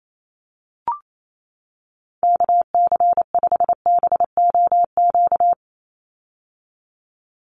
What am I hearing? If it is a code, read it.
Morse code, heard as a keyed tone, about 28 wpm, 700 Hz. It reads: KC5BOQ
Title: USA Call Signs 28wpm